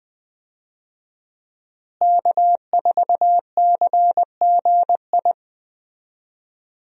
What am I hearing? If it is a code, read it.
Morse code, heard as a keyed tone, about 20 wpm, 700 Hz. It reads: K4CGI
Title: USA Call Signs - 5 Characters - Repeated 3x - 20wpm